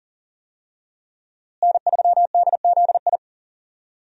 Morse code, heard as a keyed tone, about 37 wpm, 700 Hz. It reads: N3DZI
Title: USA Call Signs - 5 Characters - Repeated 2x - 40wpm